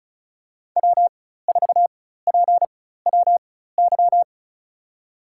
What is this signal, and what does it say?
Morse code, heard as a keyed tone, about 34 wpm, spaced as for 16 wpm, 700 Hz. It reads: W4PWY
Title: USA Call Signs - 5 Characters - 4x Character Spacing - 35wpm